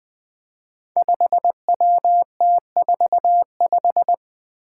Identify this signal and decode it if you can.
Morse code, heard as a keyed tone, about 20 wpm, 700 Hz. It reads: 5WT45
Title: Five Challenging Letter-Number Combinations 20wpm